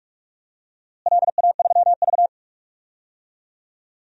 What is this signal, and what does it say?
Morse code, heard as a keyed tone, about 46 wpm, 700 Hz. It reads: LA3V
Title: DXCC-100 Call Signs - 4 Characters - Repeated 2x - 45wpm